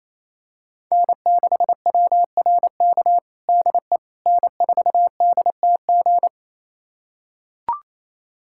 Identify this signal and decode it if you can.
Morse code, heard as a keyed tone, about 28 wpm, 700 Hz. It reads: N6WRK DE N4DTG
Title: USA Callsign DE Callsign 28wpm